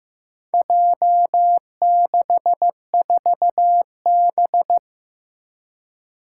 Morse code, heard as a keyed tone, about 15 wpm, 700 Hz. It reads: J64B
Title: Four Challenging Letter-Number Combinations 15wpm